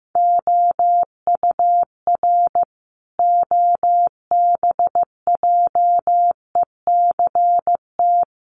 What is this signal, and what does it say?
Morse code, heard as a keyed tone, about 15 wpm, 700 Hz. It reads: OUR OBJECT
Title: Sentences from Top 1500 Words 15wpm